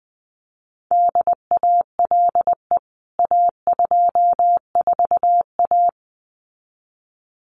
Morse code, heard as a keyed tone, about 20 wpm, 700 Hz. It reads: DALE A24A
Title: CWT Exchange 20wpm